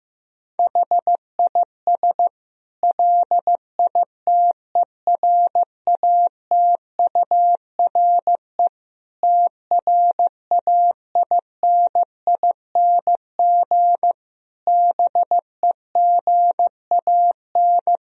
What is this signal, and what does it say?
Morse code, heard as a keyed tone, about 16 wpm, 700 Hz. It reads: HIS LITERATURE TRAINING BEGAN
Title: Sentences from Top 1000 Words - Review Part 4 - 15wpm